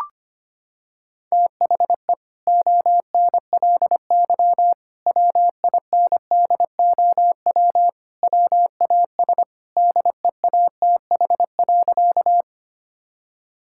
Morse code, heard as a keyed tone, about 25 wpm, 700 Hz. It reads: THE ONLY WINDOW WAS DEATH.
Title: Sentences from Top 1600 Words 25wpm